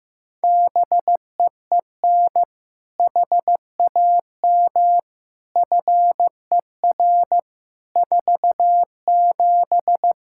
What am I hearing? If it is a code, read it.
Morse code, heard as a keyed tone, about 15 wpm, 700 Hz. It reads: BEEN HAM FER 47
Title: Practice QSO - Exchange 3 - More Info - 15wpm